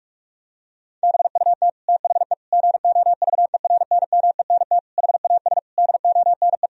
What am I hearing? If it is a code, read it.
Morse code, heard as a keyed tone, about 46 wpm, 700 Hz. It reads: BUT THE GOVERNMENT HAS DONE
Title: Sentences from Top 1000 Words - Review Part 3 - 45wpm